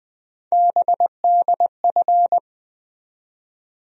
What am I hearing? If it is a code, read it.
Morse code, heard as a keyed tone, about 20 wpm, 700 Hz. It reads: BDF
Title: USA Call Sign Postfixes - 3 Letters 20wpm